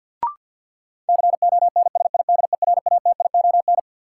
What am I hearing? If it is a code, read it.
Morse code, heard as a keyed tone, about 51 wpm, 700 Hz. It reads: CONSIDERATION
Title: Sentences from Top 1500 Words - New Word Drill Alt - 50wpm